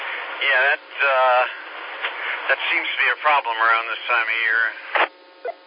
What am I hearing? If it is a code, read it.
Emergency Alert System (SAME) tones (header bursts), heard as harsh buzzing bursts, with speech. It is heard from 0.3 to 4.8 s.